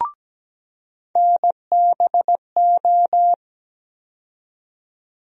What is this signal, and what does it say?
Morse code, heard as a keyed tone, about 17 wpm, 700 Hz. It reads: NBO
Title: USA Call Sign Postfixes - 3 Letters 17wpm